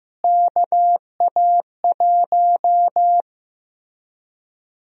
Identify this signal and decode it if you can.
Morse code, heard as a keyed tone, about 15 wpm, 700 Hz. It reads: KA1
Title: USA Call Sign Prefixes - Repeated 2x - 15wpm